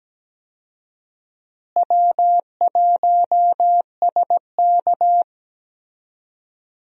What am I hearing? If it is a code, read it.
Morse code, heard as a keyed tone, about 17 wpm, 700 Hz. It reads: W1SK
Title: USA Call Signs - 4 Characters - Repeated 3x - 17wpm